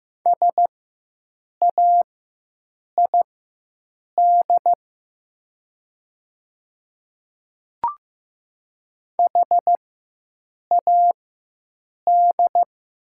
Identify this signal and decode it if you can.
Morse code, heard as a keyed tone, about 15 wpm, spaced as for 7 wpm, 700 Hz. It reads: SAID HAD
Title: 4x Character Spacing - Top 100 Words - 15wpm